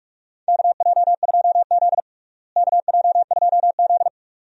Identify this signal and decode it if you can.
Morse code, heard as a keyed tone, about 46 wpm, 700 Hz. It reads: KJ2Z KJ2Z
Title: USA Call Signs - 4 Characters - Repeated 2x - 45wpm